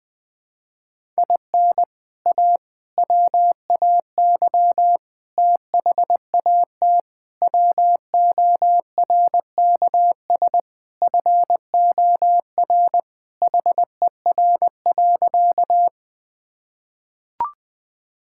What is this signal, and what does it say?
Morse code, heard as a keyed tone, about 20 wpm, 700 Hz. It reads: IN A WAY THAT WORKS FOR HER.
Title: Sentences with 4-Word Phrases - 20wpm